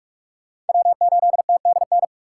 Morse code, heard as a keyed tone, about 56 wpm, 700 Hz. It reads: W8TDN
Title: Speed Racing - USA 1x3 Call Signs 45wpm